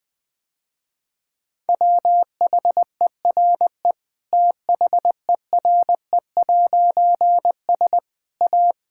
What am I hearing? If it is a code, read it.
Morse code, heard as a keyed tone, about 21 wpm, 700 Hz. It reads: WHERE THERE'S A
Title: Difficult Sentences with Hints - 20wpm